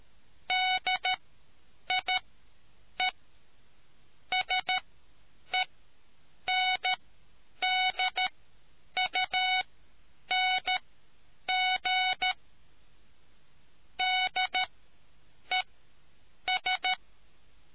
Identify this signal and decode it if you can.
Morse code, heard as a keyed tone, about 13 wpm, spaced as for 8 wpm, 740 Hz. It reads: DIE SENDUNG DES